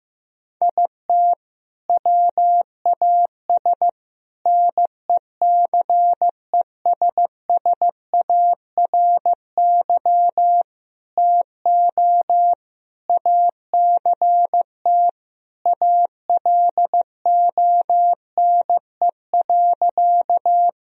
Morse code, heard as a keyed tone, about 15 wpm, 700 Hz. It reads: IT WAS NECESSARY TO ACT ALONE.
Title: Sentences from Top 1000 Words - Review Part 1 - 15wpm